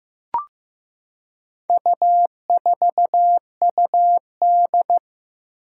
Morse code, heard as a keyed tone, about 15 wpm, 700 Hz. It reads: U4UD
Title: Four Challenging Letter-Number Combinations 15wpm